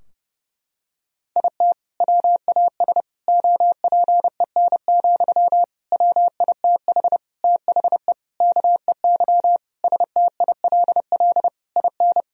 Morse code, heard as a keyed tone, about 30 wpm, 700 Hz. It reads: IT WAS OPEN, WITH THE KEY STILL IN
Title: Sentences from Top 1000 Words - Review Part 4 - 30wpm